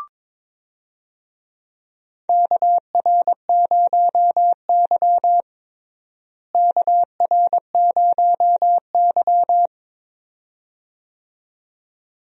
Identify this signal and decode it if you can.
Morse code, heard as a keyed tone, about 21 wpm, 700 Hz. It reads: KR0Y KR0Y
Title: USA Call Signs - 4 Characters - Repeated 2x - 22wpm